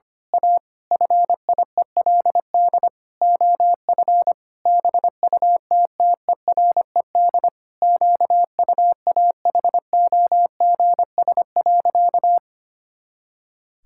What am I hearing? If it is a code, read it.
Morse code, heard as a keyed tone, about 25 wpm, 700 Hz. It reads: A FIELD OF BUTTERED QUAHOGS.
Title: Pangram Challenge v2 - 25wpm